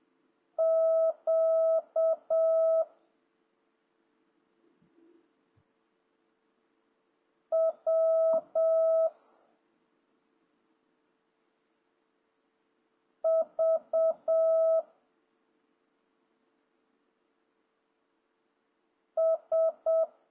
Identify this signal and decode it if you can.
Morse code, heard as a keyed tone, about 7 wpm, 650 Hz. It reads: Q W V S
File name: jh8cbh / 2026cw